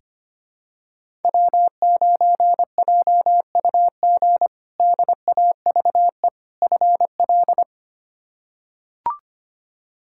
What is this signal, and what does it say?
Morse code, heard as a keyed tone, about 25 wpm, 700 Hz. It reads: W9JUG DAVE FL
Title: K1USN SST QSO Practice - 25wpm